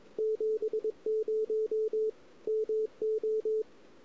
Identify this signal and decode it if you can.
Morse code, heard as a keyed tone, about 23 wpm, 410 Hz. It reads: 70 MO
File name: CW / 2008-ss-cw-k5tr-op-n5rz -01:00